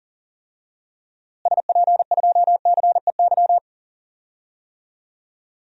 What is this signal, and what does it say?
Morse code, heard as a keyed tone, about 37 wpm, 700 Hz. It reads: SP2CEY
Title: DXCC-100 Call Signs - 6 Characters - Repeated 3x - 40wpm